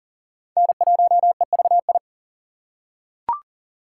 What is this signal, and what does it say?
Morse code, heard as a keyed tone, about 40 wpm, 700 Hz. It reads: N1EVI
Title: USA Call Signs - 5 Characters - 4x Character Spacing - 40wpm